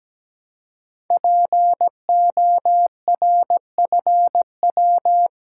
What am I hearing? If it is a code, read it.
Morse code, heard as a keyed tone, about 17 wpm, 700 Hz. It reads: PORFW
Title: Code Groups - 5 Letters - 17wpm